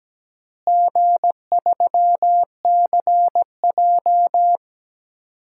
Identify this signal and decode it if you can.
Morse code, heard as a keyed tone, about 17 wpm, 700 Hz. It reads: G3CJ
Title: DXCC-100 Call Signs - 4 Characters - Repeated 3x - 17wpm